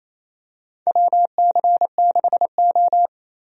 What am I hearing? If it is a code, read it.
Morse code, heard as a keyed tone, about 28 wpm, 700 Hz. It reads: WC6O
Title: USA Call Signs - 4 Characters - 28wpm